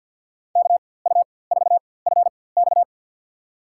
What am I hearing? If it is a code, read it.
Morse code, heard as a keyed tone, about 47 wpm, spaced as for 23 wpm, 700 Hz. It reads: KU4FX
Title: USA Call Signs - 4x Character Spacing - 50wpm